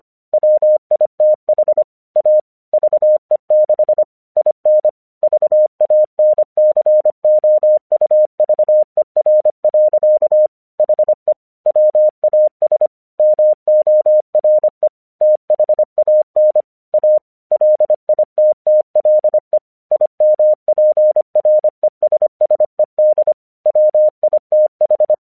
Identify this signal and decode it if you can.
Morse code, heard as a keyed tone, about 25 wpm, 610 Hz. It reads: WITH A VE6 IN VANCOUVER. HE WAS MORE THAN A LITTLE IMPRESSED WITH
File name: 1 English 25wpm / AQC29